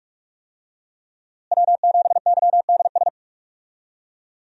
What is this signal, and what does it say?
Morse code, heard as a keyed tone, about 46 wpm, 700 Hz. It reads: W7YDS
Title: USA Call Signs - 5 Characters - Repeated 3x - 45wpm